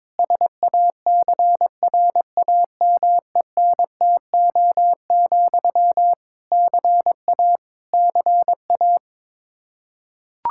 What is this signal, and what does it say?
Morse code, heard as a keyed tone, about 22 wpm, 700 Hz. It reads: SACRAMENTO, CA CA
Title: K1USN SST QSO Practice 22wpm